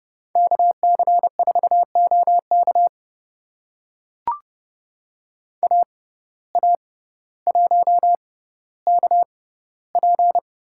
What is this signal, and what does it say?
Morse code, heard as a keyed tone, about 30 wpm, spaced as for 15 wpm, 700 Hz. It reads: KC4OK AA1KP
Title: USA Call Signs - 5 Characters - 6x Character Spacing - 30wpm